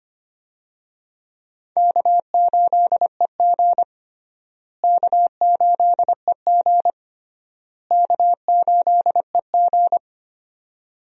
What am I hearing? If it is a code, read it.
Morse code, heard as a keyed tone, about 25 wpm, 700 Hz. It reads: K8EG K8EG K8EG
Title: USA Call Signs - 4 Characters - Repeated 3x - 25wpm